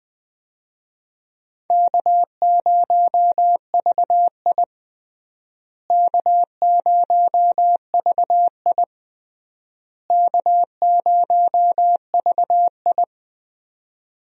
Morse code, heard as a keyed tone, about 21 wpm, 700 Hz. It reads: K0VI K0VI K0VI
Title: USA Call Signs - 4 Characters - Repeated 3x - 20wpm